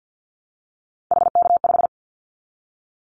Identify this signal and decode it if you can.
Morse code, heard as a keyed tone, about 50 wpm, 700 Hz. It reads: HK5
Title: Callsign Prefixes for Top 100 DXCC - 50wpm